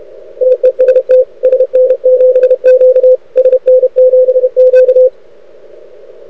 Morse code, heard as a keyed tone, about 30 wpm, 500 Hz. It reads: TEST SN7Q SN7Q
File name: rec contest / arrl cw 2018 / test 32